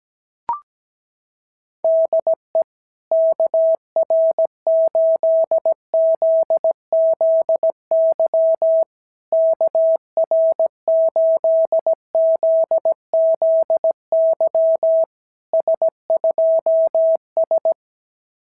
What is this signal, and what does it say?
Morse code, heard as a keyed tone, about 17 wpm, 660 Hz. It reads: DE KR8ZZY KR8ZZY S2S
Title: SOTA QSO Practice 17wpm